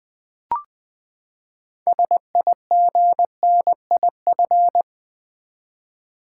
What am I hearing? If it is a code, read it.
Morse code, heard as a keyed tone, about 20 wpm, 700 Hz. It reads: SIGNIF
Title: Build A Word - 6 Letters At A Time - 12-Letter Words - 20wpm